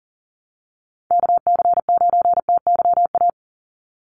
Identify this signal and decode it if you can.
Morse code, heard as a keyed tone, about 40 wpm, 700 Hz. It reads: KC9TYA
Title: NA SOTA Chaser Callsigns 40wpm